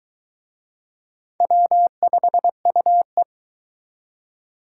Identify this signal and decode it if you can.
Morse code, heard as a keyed tone, about 23 wpm, 700 Hz. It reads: W5UE